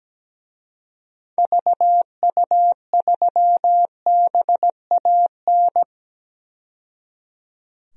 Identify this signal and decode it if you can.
Morse code, heard as a keyed tone, about 17 wpm, 700 Hz. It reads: VU3BAN
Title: DXCC-100 Call Signs - 6 Characters - 17wpm